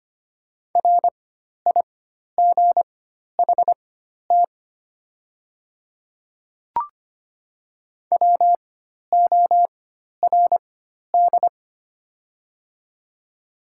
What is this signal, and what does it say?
Morse code, heard as a keyed tone, about 25 wpm, spaced as for 12 wpm, 690 Hz. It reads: RIGHT WORD
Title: 4x Character Spacing - Top 200 Words - 25wpm